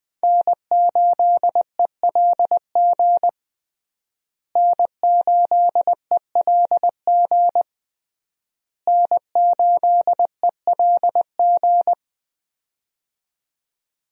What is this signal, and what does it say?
Morse code, heard as a keyed tone, about 21 wpm, 700 Hz. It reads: N8ELG N8ELG N8ELG
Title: USA Call Signs - 5 Characters - Repeated 3x - 20wpm